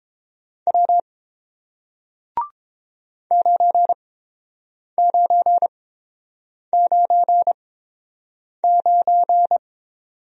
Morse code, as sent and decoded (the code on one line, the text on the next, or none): .-- / ----. ----. ----. ----.
W 9999